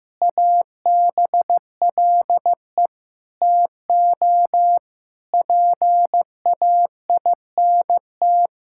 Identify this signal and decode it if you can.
Morse code, heard as a keyed tone, about 15 wpm, 700 Hz. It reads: ABLE TO PAINT
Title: Sentences from Top 1000 Words - Review Part 6 - 15wpm